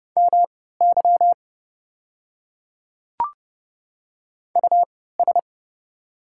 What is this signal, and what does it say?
Morse code, heard as a keyed tone, about 31 wpm, spaced as for 17 wpm, 700 Hz. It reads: MY US
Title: 3x Character Spacing - Top 200 Words - 30wpm